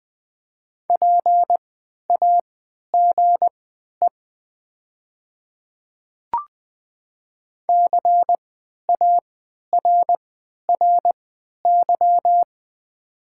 Morse code, heard as a keyed tone, about 20 wpm, spaced as for 11 wpm, 700 Hz. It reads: PAGE CARRY
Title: 3x Character Spacing - Top 300 Words - 20wpm